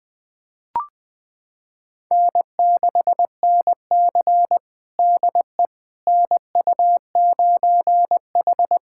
Morse code, heard as a keyed tone, about 20 wpm, 700 Hz. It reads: N6NC DE NU9H
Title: USA Callsign DE Callsign - 4 Characters - 20wpm